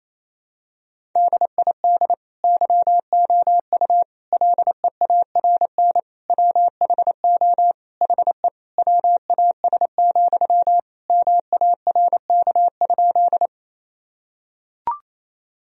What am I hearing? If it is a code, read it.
Morse code, heard as a keyed tone, about 28 wpm, 700 Hz. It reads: DID YOU LEARN WHO HE WAS, MARK?
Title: Sentences from Top 1000 Words - Review Part 1 - 28wpm